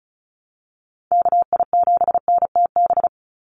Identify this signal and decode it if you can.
Morse code, heard as a keyed tone, about 34 wpm, 700 Hz. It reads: KI7NTB
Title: USA Call Signs 35wpm